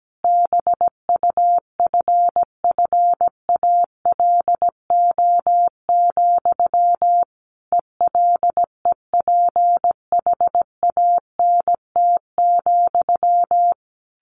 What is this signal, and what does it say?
Morse code, heard as a keyed tone, about 17 wpm, 700 Hz. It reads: BUFFALO, ELEPHANT,
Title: A Sound Of Thunder 17wpm